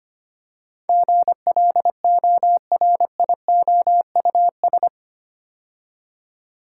Morse code, heard as a keyed tone, about 25 wpm, 700 Hz. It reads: GLORIOUS
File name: 8 Letter Words - First 3 Letters Given - 25wpm